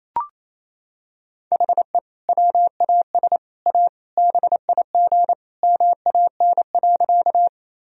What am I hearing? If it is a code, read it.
Morse code, heard as a keyed tone, about 28 wpm, 700 Hz. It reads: HE WAS A BIG MAN.
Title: Sentences from Top 1000 Words - Review Part 1 - 28wpm